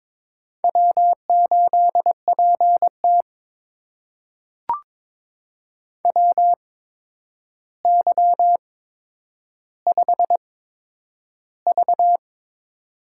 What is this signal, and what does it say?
Morse code, heard as a keyed tone, about 22 wpm, spaced as for 9 wpm, 700 Hz. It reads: W8PT WY5V
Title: USA Call Signs - 4 Characters - 8x Character Spacing - 22wpm